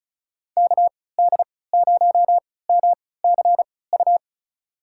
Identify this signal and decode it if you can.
Morse code, heard as a keyed tone, about 36 wpm, spaced as for 20 wpm, 700 Hz. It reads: KD0MCU
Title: USA Call Signs - 6 Characters - 3x Character Spacing - 35wpm